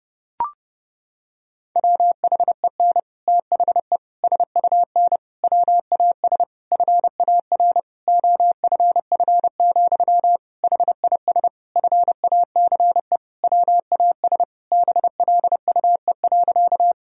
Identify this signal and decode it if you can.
Morse code, heard as a keyed tone, about 32 wpm, 700 Hz. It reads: WHEN THE SUN WAS FAR OFF, HIS FACE WAS BLUE.
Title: Sentences from Top 1000 Words - Review Part 3 - 30wpm